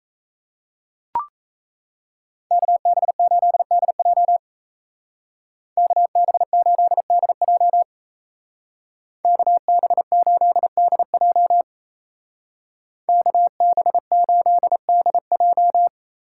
Morse code, heard as a keyed tone, about 36 wpm, 700 Hz. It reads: KB8DJ KB8DJ KB8DJ KB8DJ